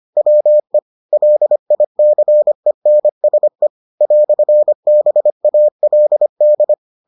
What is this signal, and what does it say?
Morse code, heard as a keyed tone, about 25 wpm, 600 Hz. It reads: WE LICENSE "BALD